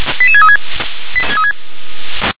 Five-tone selective call (ZVEI), heard as quick runs of five tones, tones of about 70 ms. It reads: 88636
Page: /357/SDR/ZVEI/